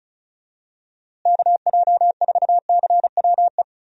Morse code, heard as a keyed tone, about 36 wpm, 700 Hz. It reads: KJ4CWE
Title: USA Call Signs - 6 Characters - Repeated 2x - 35wpm